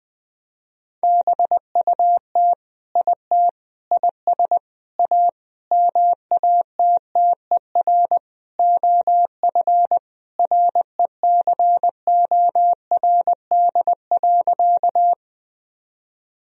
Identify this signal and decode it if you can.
Morse code, heard as a keyed tone, about 20 wpm, 700 Hz. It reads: BUT IT IS A MATTER OF RECORD.
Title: Sentences from Top 1000 Words - Review Part 3 - 20wpm